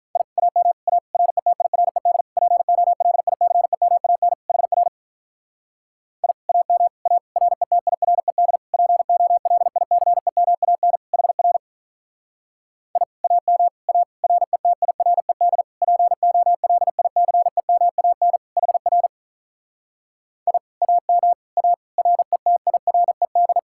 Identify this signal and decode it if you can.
Morse code, heard as a keyed tone, about 44 wpm, 700 Hz. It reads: I AM A RETIRED POLICEMAN HR I AM A RETIRED POLICEMAN HR I AM A RETIRED POLICEMAN HR I AM A RETIRED
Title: Speed-Racing Common QSO Phrases - 35wpm